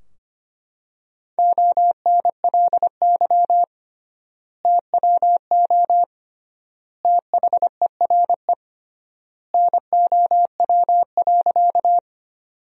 Morse code, heard as a keyed tone, about 25 wpm, 700 Hz. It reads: ONLY TWO THERE NOW.
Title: 3x Word-Spacing - Sentences from Top 200 Words - 25wpm